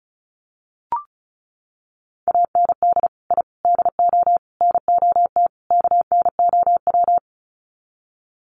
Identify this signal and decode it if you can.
Morse code, heard as a keyed tone, about 34 wpm, 700 Hz. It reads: AND I DO NOT KNOW
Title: Sets of 5 Words 35wpm